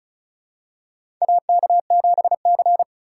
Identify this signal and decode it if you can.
Morse code, heard as a keyed tone, about 36 wpm, 700 Hz. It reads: AK7C
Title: USA Call Signs - 4 Characters - 2x Character Spacing - 35wpm